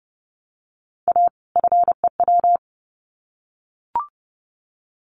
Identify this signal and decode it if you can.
Morse code, heard as a keyed tone, about 30 wpm, 700 Hz. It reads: A FEW
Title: Sets of 2 Words 30wpm